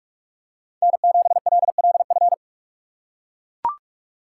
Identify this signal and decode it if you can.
Morse code, heard as a keyed tone, about 44 wpm, 700 Hz. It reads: N7LLF